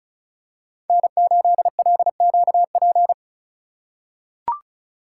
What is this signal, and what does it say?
Morse code, heard as a keyed tone, about 34 wpm, 690 Hz. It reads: N8LQP
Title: USA Call Signs - 5 Characters - 2x Character Spacing - 35wpm